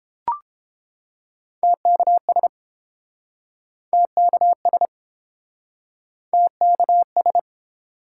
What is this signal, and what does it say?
Morse code, heard as a keyed tone, about 28 wpm, 700 Hz. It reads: TKS TKS TKS